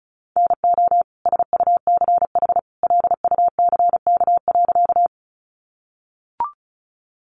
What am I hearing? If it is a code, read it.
Morse code, heard as a keyed tone, about 34 wpm, 700 Hz. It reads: NO SUCH LUCK.
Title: Sentences from Top 1500 Words 35wpm